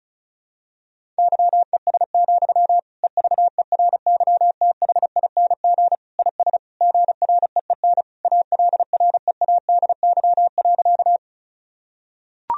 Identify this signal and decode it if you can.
Morse code, heard as a keyed tone, about 34 wpm, 700 Hz. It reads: YES, EVERYTHING IS GREEN ALREADY.
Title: Sentences from Top 1000 Words - Review Part 2 - 35wpm